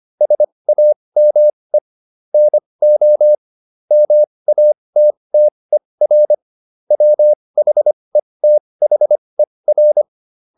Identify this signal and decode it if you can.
Morse code, heard as a keyed tone, about 25 wpm, spaced as for 20 wpm, 600 Hz. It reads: SAME NO MATTER WHETHER